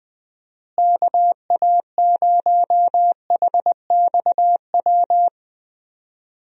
Morse code, heard as a keyed tone, about 20 wpm, 700 Hz. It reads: KA0HXW
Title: USA Call Signs - 6 Characters - Repeated 2x - 20wpm